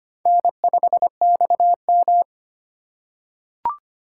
Morse code, heard as a keyed tone, about 25 wpm, 700 Hz. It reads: N5XM
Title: USA Call Signs - 4 Characters - 2x Character Spacing - 25wpm